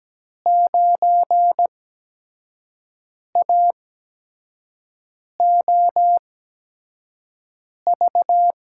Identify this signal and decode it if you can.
Morse code, heard as a keyed tone, about 17 wpm, spaced as for 5 wpm, 700 Hz. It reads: 9AOV
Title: USA Call Signs - 5 Characters - 8x Character Spacing - 17wpm